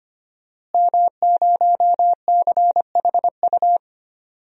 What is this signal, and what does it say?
Morse code, heard as a keyed tone, about 25 wpm, 700 Hz. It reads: M0CHU
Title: DXCC-100 Call Signs - 5 Characters - Repeated 2x - 25wpm